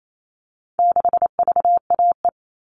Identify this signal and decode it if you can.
Morse code, heard as a keyed tone, about 28 wpm, 700 Hz. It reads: 6VAE
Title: All Cut Numbers - 4-Digit 28wpm